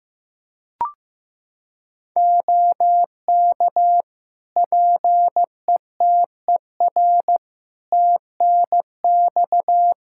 Morse code, heard as a keyed tone, about 15 wpm, 700 Hz. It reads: OK PETER TNX
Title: K1USN SST QSO Practice 15wpm